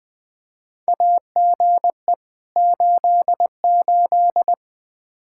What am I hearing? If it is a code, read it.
Morse code, heard as a keyed tone, about 20 wpm, 700 Hz. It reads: AGE 88
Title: Practice QSO - Exchange 3 - More Info - 20wpm